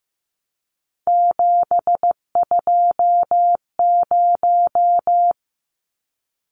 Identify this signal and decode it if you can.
Morse code, heard as a keyed tone, about 15 wpm, 700 Hz. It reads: 720
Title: Numbers - 3-Digit 15wpm